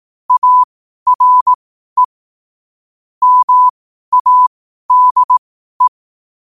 Morse code, heard as a keyed tone, about 18 wpm, spaced as for 12 wpm, 1000 Hz. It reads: ARE MADE
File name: qotdfiles / 12wpm